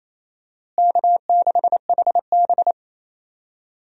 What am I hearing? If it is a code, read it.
Morse code, heard as a keyed tone, about 28 wpm, 700 Hz. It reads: K6HB
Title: USA Call Signs - 4 Characters - Repeated 2x - 28wpm